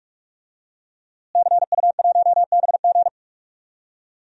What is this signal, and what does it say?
Morse code, heard as a keyed tone, about 42 wpm, 700 Hz. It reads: CU1BG
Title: DXCC-100 Call Signs - 5 Characters - Repeated 2x - 45wpm